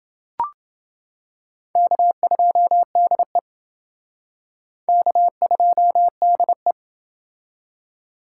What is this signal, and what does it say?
Morse code, heard as a keyed tone, about 28 wpm, 700 Hz. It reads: K2DE K2DE